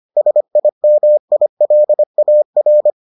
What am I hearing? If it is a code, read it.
Morse code, heard as a keyed tone, about 25 wpm, 600 Hz. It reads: SIMILAR